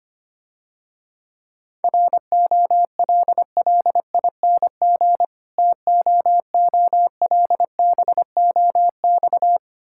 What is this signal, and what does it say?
Morse code, heard as a keyed tone, about 26 wpm, 700 Hz. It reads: ROLLING TOOLBOX
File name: practice / 25wpm / Difficult Words - Speed-Racing with Hints - 25wpm